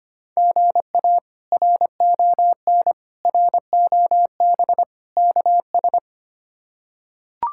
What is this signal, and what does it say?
Morse code, heard as a keyed tone, about 25 wpm, 700 Hz. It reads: GA RON ROB KS
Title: K1USN SST QSO Practice - 25wpm